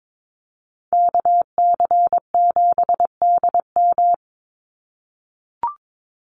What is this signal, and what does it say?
Morse code, heard as a keyed tone, about 22 wpm, 700 Hz. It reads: KC7DM
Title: NA SOTA Activator Callsigns 22wpm